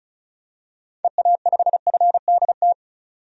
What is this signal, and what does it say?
Morse code, heard as a keyed tone, about 34 wpm, 700 Hz. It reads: EA5FDT
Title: DXCC-100 Call Signs - 6 Characters - Repeated 2x - 35wpm